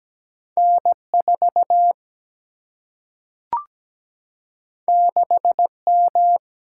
Morse code, heard as a keyed tone, about 17 wpm, 700 Hz. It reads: N 4 6 M
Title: Single-Letter Single-Number 17wpm